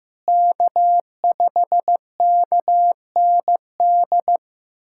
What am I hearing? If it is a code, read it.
Morse code, heard as a keyed tone, about 16 wpm, 700 Hz. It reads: K5KND
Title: K1USN SST QSO Practice - 15wpm